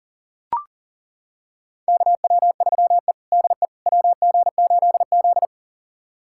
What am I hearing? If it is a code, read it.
Morse code, heard as a keyed tone, about 37 wpm, 700 Hz. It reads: KW3E DE WG8Z
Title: USA Callsign DE Callsign - 4 Characters - 40wpm